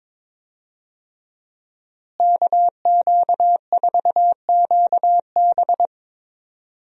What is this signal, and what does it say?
Morse code, heard as a keyed tone, about 22 wpm, 700 Hz. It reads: KQ4QB
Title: USA Call Signs - 5 Characters - Repeated 2x - 22wpm